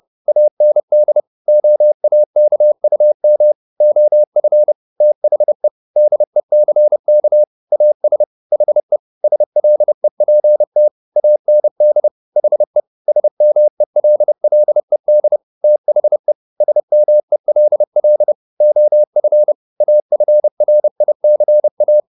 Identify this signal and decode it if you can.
Morse code, heard as a keyed tone, about 30 wpm, 600 Hz. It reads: AND OAKUM OF THE DECK AS HE SLEPT AND HE SMELLED THE SMELL OF AFRICA
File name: the old man and the sea / the old man and the sea 1 30wpm